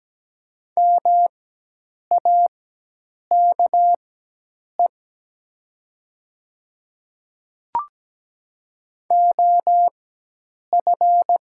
Morse code, heard as a keyed tone, about 17 wpm, spaced as for 8 wpm, 700 Hz. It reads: MAKE OF
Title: 4x Character Spacing - Top 100 Words - 17wpm